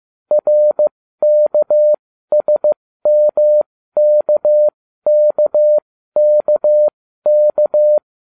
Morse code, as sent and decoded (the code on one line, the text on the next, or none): .-. -.- ... -- -.- -.- -.- -.-
RKSMKKKK